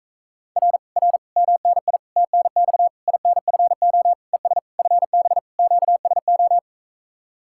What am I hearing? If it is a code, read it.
Morse code, heard as a keyed tone, about 44 wpm, 700 Hz. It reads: R R MNI TNX INFO ES FB QSO